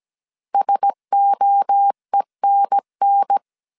Morse code, heard as a keyed tone, about 17 wpm, 800 Hz. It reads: SOENN
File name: 16wpm